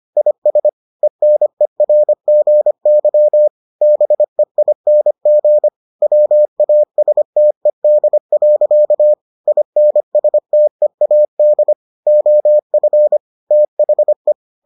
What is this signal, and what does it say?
Morse code, heard as a keyed tone, about 25 wpm, 600 Hz. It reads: IS ENERGY BEING WASTED. INSTEAD OF THE